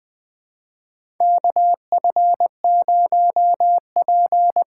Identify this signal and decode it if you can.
Morse code, heard as a keyed tone, about 20 wpm, 700 Hz. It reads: KF0P
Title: USA Call Signs - 5 Characters - 6x Character Spacing - 20wpm